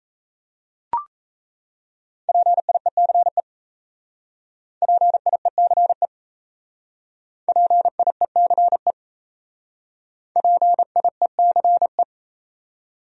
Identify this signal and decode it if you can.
Morse code, heard as a keyed tone, about 34 wpm, 700 Hz. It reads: PIECE PIECE PIECE PIECE
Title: Speed-Racing Top 500 Words 28wpm